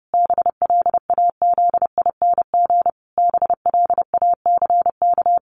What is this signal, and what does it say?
Morse code, heard as a keyed tone, about 30 wpm, 700 Hz. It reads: BLAZING BLACK
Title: A Sound Of Thunder 30wpm